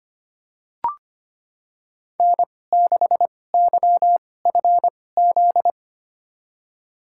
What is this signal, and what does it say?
Morse code, heard as a keyed tone, about 25 wpm, spaced as for 18 wpm, 700 Hz. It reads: N6YFZ
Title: USA Call Signs - 2x Character Spacing - 25wpm